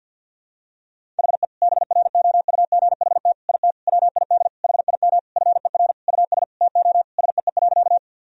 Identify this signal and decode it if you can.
Morse code, heard as a keyed tone, about 53 wpm, 700 Hz. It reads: HE BROUGHT IT WID HIM FER US TO SEE.
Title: Sentences from Top 500 Words and Abbreviations 50wpm